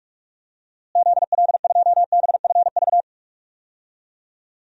Morse code, heard as a keyed tone, about 46 wpm, 700 Hz. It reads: ZL2BFV
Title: DXCC-100 Call Signs - 6 Characters - Repeated 3x - 45wpm